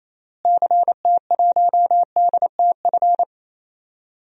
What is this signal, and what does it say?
Morse code, heard as a keyed tone, about 28 wpm, 700 Hz. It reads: CT1DTF
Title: DXCC-100 Call Signs - 6 Characters - Repeated 2x - 28wpm